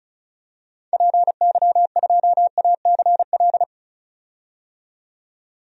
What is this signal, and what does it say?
Morse code, heard as a keyed tone, about 36 wpm, 700 Hz. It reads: PY2ACL
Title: DXCC-100 Call Signs - 6 Characters - 35wpm